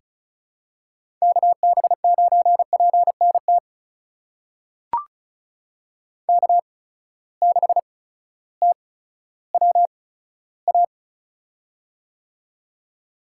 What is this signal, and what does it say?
Morse code, heard as a keyed tone, about 36 wpm, spaced as for 16 wpm, 700 Hz. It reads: KB9PNT K6TWA
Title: USA Call Signs - 8x Character Spacing - 35wpm